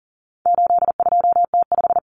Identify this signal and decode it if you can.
Morse code, heard as a keyed tone, about 40 wpm, 700 Hz. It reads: 82T5
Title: Common Cut Numbers - 4-Digit 40wpm